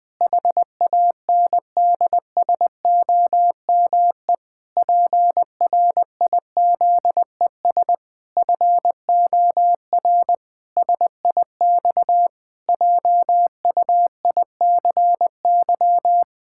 Morse code, heard as a keyed tone, about 19 wpm, 700 Hz. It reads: HANDSOME PRIZES FOR SIX JUICY
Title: Pangram Challenge v2 - 20wpm